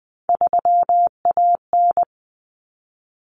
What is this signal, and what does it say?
Morse code, heard as a keyed tone, about 20 wpm, 700 Hz. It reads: 3AN
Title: Common Cut Numbers - 3-Digit 20wpm